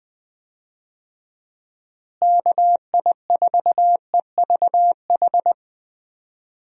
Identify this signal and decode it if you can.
Morse code, heard as a keyed tone, about 20 wpm, 700 Hz. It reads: KI4EVH
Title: USA Call Signs - 6 Characters - Repeated 3x - 20wpm